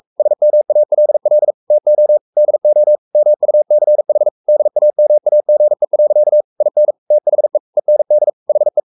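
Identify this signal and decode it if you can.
Morse code, heard as a keyed tone, about 44 wpm, 600 Hz. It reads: SMALL TO DO MUCH DAMAGE. IN THE END HE
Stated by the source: Morse code at 43 wpm sending Ken Follett's novel Never in English, from CW med Gnister.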